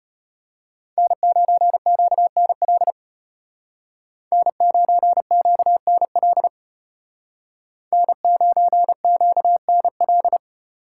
Morse code, heard as a keyed tone, about 32 wpm, 700 Hz. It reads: N9QNL N9QNL N9QNL